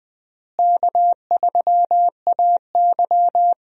Morse code, heard as a keyed tone, about 20 wpm, 700 Hz. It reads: K3AY
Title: USA Call Signs - 4 Characters - Repeated 2x - 20wpm